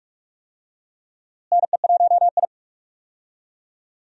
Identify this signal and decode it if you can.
Morse code, heard as a keyed tone, about 47 wpm, 700 Hz. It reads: NE1I
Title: USA Call Signs - 4 Characters - Repeated 2x - 45wpm